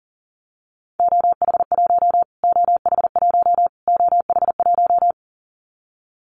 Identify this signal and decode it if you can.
Morse code, heard as a keyed tone, about 40 wpm, 700 Hz. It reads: OH1 OH1 OH1
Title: Callsign Prefixes for Top 25 DXCC - Repeated 3x - 40wpm